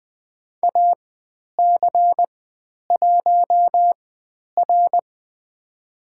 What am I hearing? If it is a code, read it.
Morse code, heard as a keyed tone, about 20 wpm, spaced as for 10 wpm, 700 Hz. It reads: AC1R
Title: 109 Lesson 18g - Letter m - Call Signs 20wpm